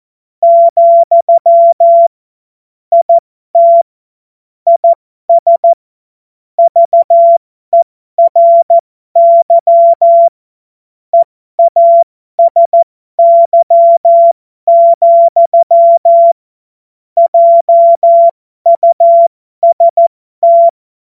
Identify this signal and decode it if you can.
Morse code, heard as a keyed tone, about 14 wpm, spaced as for 12 wpm, 680 Hz. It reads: , IT IS VERY EASY, JUST
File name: Morsekurs / Witze